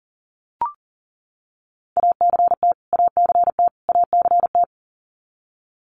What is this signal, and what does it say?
Morse code, heard as a keyed tone, about 40 wpm, 700 Hz. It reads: ACT ACT ACT
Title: Top 500 Words Repeated 40wpm